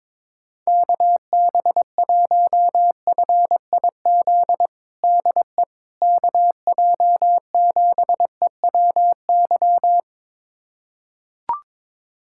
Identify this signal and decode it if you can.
Morse code, heard as a keyed tone, about 22 wpm, 700 Hz. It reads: KB1FIZ DE KJ7EWY
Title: USA Callsign DE Callsign 22wpm